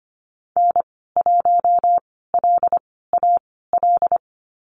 Morse code, heard as a keyed tone, about 25 wpm, spaced as for 16 wpm, 700 Hz. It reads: N1LAL
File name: Lesson 15g - Number 5 - Call Signs 25wpm